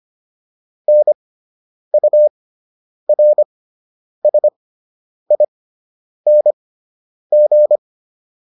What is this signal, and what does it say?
Morse code, heard as a keyed tone, about 25 wpm, spaced as for 9 wpm, 600 Hz. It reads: NURSING